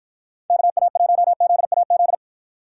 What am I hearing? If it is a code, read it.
Morse code, heard as a keyed tone, about 56 wpm, 700 Hz. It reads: KA1ZAZ